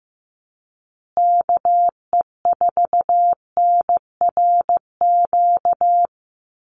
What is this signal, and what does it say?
Morse code, heard as a keyed tone, about 15 wpm, 700 Hz. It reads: KE4NRQ
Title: USA Call Signs Repeated 2x 15wpm